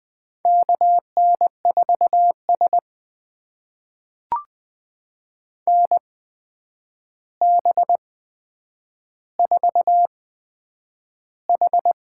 Morse code, heard as a keyed tone, about 20 wpm, spaced as for 9 wpm, 700 Hz. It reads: KN4S NB4H
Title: USA Call Signs - 4 Characters - 8x Character Spacing - 20wpm